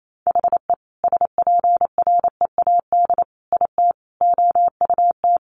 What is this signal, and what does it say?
Morse code, heard as a keyed tone, about 28 wpm, 700 Hz. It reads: HE SPREAD IT OUT
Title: Sentences from Top 1200 Words 28wpm